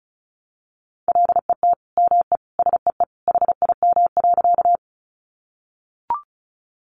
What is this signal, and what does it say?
Morse code, heard as a keyed tone, about 34 wpm, 700 Hz. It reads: LET ME SEE HIM.
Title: Sentences from Top 300 Words 35wpm